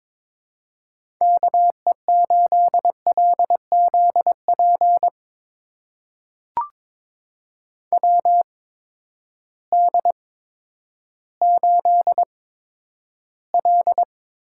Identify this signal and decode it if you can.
Morse code, heard as a keyed tone, about 22 wpm, spaced as for 11 wpm, 700 Hz. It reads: KE8LZP WD8L
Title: USA Call Signs - 6 Characters - 8x Character Spacing - 22wpm